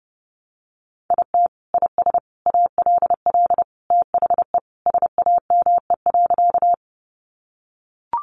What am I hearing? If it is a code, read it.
Morse code, heard as a keyed tone, about 30 wpm, 700 Hz. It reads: IT IS ALL THE SAME.
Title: Sentences from Top 200 Words 30wpm